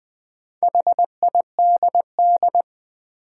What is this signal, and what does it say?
Morse code, heard as a keyed tone, about 20 wpm, 700 Hz. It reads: HIDD
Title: Build A Word - 2 Letters At A Time - 6-Letter Words - 20wpm